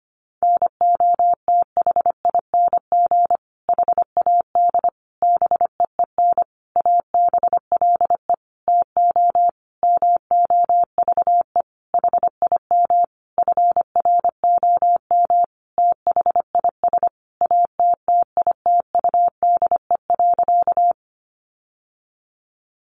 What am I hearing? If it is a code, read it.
Morse code, heard as a keyed tone, about 25 wpm, 700 Hz. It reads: NOTHING HAD BEEN ABLE TO MOVE HIM FROM THIS ATTITUDE.
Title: Sentences from Top 1300 Words 25wpm